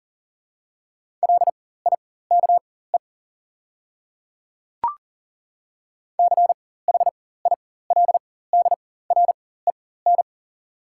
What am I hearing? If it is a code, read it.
Morse code, heard as a keyed tone, about 40 wpm, spaced as for 19 wpm, 690 Hz. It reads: LIKE CHILDREN
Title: 4x Character Spacing - Top 300 Words - 40wpm